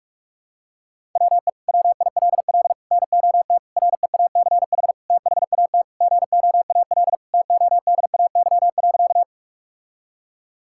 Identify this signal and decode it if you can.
Morse code, heard as a keyed tone, about 46 wpm, 700 Hz. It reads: WE WILL NOT REACH THAT GOAL TODAY.
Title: Sentences from Top 1000 Words - Review Part 5 - 45wpm